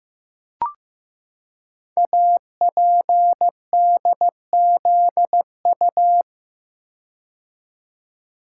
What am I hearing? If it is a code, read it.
Morse code, heard as a keyed tone, about 15 wpm, 700 Hz. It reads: APDZU
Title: Code Groups - 5 Letters - 15wpm